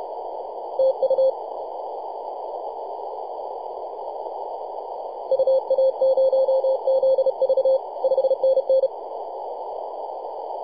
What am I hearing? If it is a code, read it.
Morse code, heard as a keyed tone, about 32 wpm, 530 Hz, with noise. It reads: TU UA0ZV 5NN
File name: posts / 2015-oct-03-2319z-tx3x-21-023mhz-cw